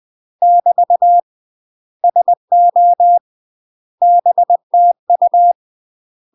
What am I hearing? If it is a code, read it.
Morse code, heard as a keyed tone, about 20 wpm, 700 Hz. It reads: <BT> SO BTU
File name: UK MASTER QSOs TRAINING / 20WPM / UK MASTER QSOs 0030